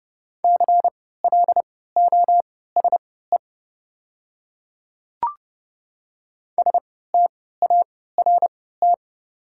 Morse code, heard as a keyed tone, about 30 wpm, spaced as for 17 wpm, 690 Hz. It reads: CLOSE START
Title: 3x Character Spacing - Top 300 Words - 30wpm